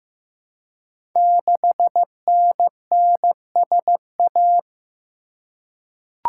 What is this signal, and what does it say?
Morse code, heard as a keyed tone, about 15 wpm, 700 Hz. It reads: 6NNSA
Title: Code Groups - 5 Letter:Numbers - 15wpm